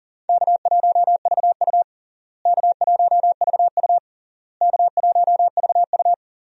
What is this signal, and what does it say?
Morse code, heard as a keyed tone, about 40 wpm, 700 Hz. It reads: K1VU K1VU K1VU
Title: USA Call Signs - 4 Characters - Repeated 3x - 40wpm